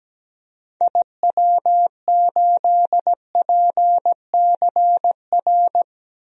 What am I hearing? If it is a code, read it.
Morse code, heard as a keyed tone, about 17 wpm, 700 Hz. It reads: IW8PCR
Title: DXCC-100 Call Signs - 6 Characters - Repeated 3x - 17wpm